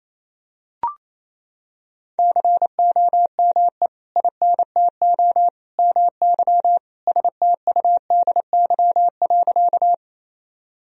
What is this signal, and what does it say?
Morse code, heard as a keyed tone, about 28 wpm, 700 Hz. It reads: COME INTO MY STUDY.
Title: Sentences from Top 1000 Words - Review Part 1 - 28wpm